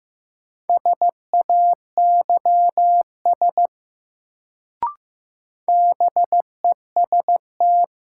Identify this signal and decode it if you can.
Morse code, heard as a keyed tone, about 15 wpm, 700 Hz. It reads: SAYS BEST
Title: Sets of 2 Words - New Word Drill - 15wpm